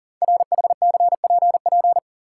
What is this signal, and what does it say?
Morse code, heard as a keyed tone, about 40 wpm, 690 Hz. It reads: RHCPP
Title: Code Groups - 5 Letters - 40wpm